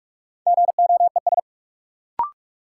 Morse code, heard as a keyed tone, about 47 wpm, 700 Hz. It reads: GOES
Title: Sentences from Top 1000 Words - Review Part 5 - New Word Drill - 45wpm